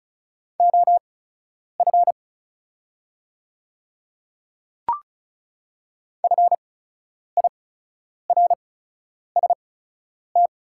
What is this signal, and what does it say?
Morse code, heard as a keyed tone, about 36 wpm, spaced as for 10 wpm, 700 Hz. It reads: OF FIRST
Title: 8x Character Spacing - Top 100 Words - 35wpm